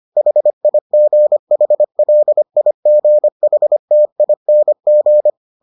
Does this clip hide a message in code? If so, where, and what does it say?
Yes: Morse code from 0.2 to 5.3 s, heard as a keyed tone, about 25 wpm, 600 Hz. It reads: HIGHLIGHTING